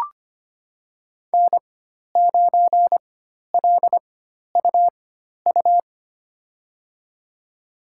Morse code, heard as a keyed tone, about 25 wpm, spaced as for 12 wpm, 700 Hz. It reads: N9LUU
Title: USA Call Signs - 5 Characters - 4x Character Spacing - 25wpm